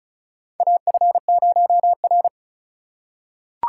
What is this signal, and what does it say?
Morse code, heard as a keyed tone, about 33 wpm, 700 Hz. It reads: AF0R
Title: USA Call Signs - 4 Characters - 4x Character Spacing - 35wpm